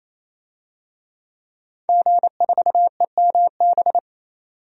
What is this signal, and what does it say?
Morse code, heard as a keyed tone, about 28 wpm, 700 Hz. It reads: G4EMB
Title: DXCC-100 Call Signs - 5 Characters - 28wpm